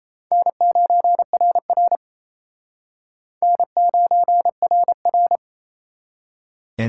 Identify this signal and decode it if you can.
Morse code, heard as a keyed tone, about 34 wpm, 700 Hz. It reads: N9RR N9RR
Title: Speed Racing - USA 1x2 Call Signs 28wpm